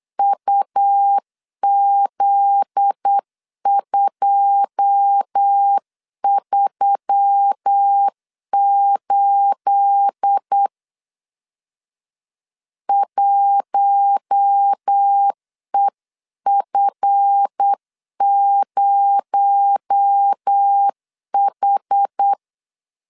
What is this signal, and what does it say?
Morse code, heard as a keyed tone, about 8 wpm, 800 Hz. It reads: UZ238 1EF0H
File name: LEZIONI / 9 28 8wpm 3512